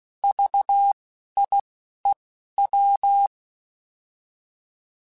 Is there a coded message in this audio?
Yes: Morse code from 0.2 to 3.3 s, heard as a keyed tone, about 16 wpm, spaced as for 11 wpm, 800 Hz. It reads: VIEW